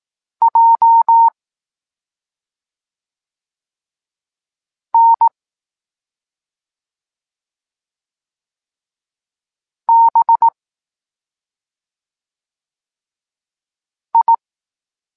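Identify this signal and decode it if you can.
Morse code, heard as a keyed tone, about 18 wpm, 920 Hz. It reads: J N B I
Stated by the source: Evenly spaced Morse lesson text 21 for learning letters G and Z, from SP9MOA Niepołomicki Klub Krótkofalowców.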